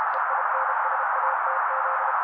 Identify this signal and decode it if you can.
Morse code, heard as a keyed tone, about 32 wpm, 530 Hz, with noise. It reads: EINSATZ
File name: daily / 31wpm noise